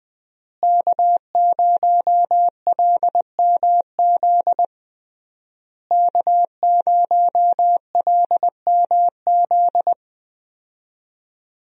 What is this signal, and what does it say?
Morse code, heard as a keyed tone, about 20 wpm, 700 Hz. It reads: K0LMZ K0LMZ
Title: USA Call Signs - 5 Characters - Repeated 3x - 20wpm